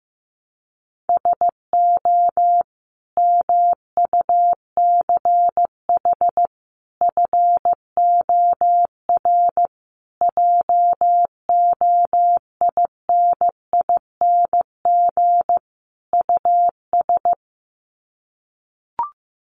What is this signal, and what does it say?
Morse code, heard as a keyed tone, about 14 wpm, 700 Hz. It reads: SO MUCH FOR JOINING US
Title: Sets of 5 Words 15wpm